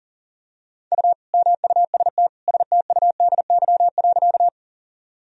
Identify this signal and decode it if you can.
Morse code, heard as a keyed tone, about 40 wpm, 700 Hz. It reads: U MUST STUDY.
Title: Sentences from Top 300 Words and Abbreviations 40wpm